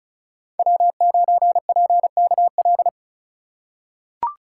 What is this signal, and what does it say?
Morse code, heard as a keyed tone, about 36 wpm, 700 Hz. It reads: W9PKL
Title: USA Call Signs - 5 Characters - 2x Character Spacing - 35wpm